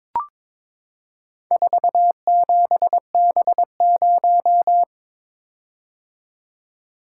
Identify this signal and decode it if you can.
Morse code, heard as a keyed tone, about 22 wpm, 700 Hz. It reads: 47B0
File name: practice / 22wpm / Four Letter-Number Combinations - 22wpm